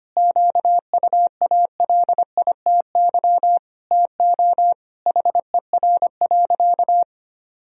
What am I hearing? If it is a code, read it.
Morse code, heard as a keyed tone, about 25 wpm, 700 Hz. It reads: QUALITY TO HER.
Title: Sentences from Top 1000 Words - Review Part 4 - 25wpm